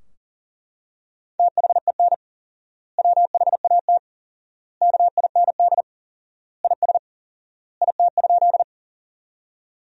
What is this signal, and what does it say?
Morse code, heard as a keyed tone, about 40 wpm, 700 Hz. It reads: THEN WHAT KIND IS IT?
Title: 4x Word-Spacing - Sentences from Top 200 Words - 40wpm